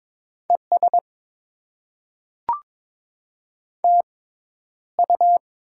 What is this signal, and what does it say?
Morse code, heard as a keyed tone, about 22 wpm, spaced as for 11 wpm, 700 Hz. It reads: ES TU
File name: Top 90 QSO Elements - 6x Character Spacing - 22wpm